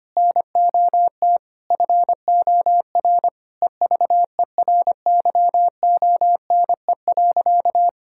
Morse code, heard as a keyed tone, about 25 wpm, 700 Hz. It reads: NOT FOR EVERYONE.
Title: Sentences from Top 1000 Words - Review Part 5 - 25wpm